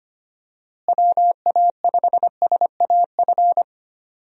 Morse code, heard as a keyed tone, about 25 wpm, 700 Hz. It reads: WA5SAF
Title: USA Call Signs - 6 Characters - Repeated 2x - 25wpm